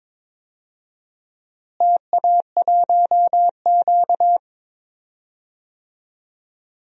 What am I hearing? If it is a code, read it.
Morse code, heard as a keyed tone, about 21 wpm, 700 Hz. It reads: TA1Q
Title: DXCC-100 Call Signs - 4 Characters - Repeated 2x - 22wpm